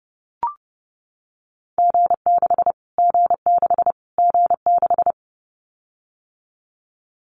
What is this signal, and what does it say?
Morse code, heard as a keyed tone, about 30 wpm, 690 Hz. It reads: G6 G6 G6
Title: Callsign Prefixes for Top 25 DXCC - Repeated 3x - 30wpm